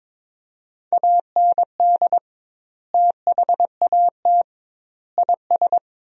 Morse code, heard as a keyed tone, about 22 wpm, 700 Hz. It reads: AND THAT IS
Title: 2x Word-Spacing - Sentences from Top 400 Words - 22wpm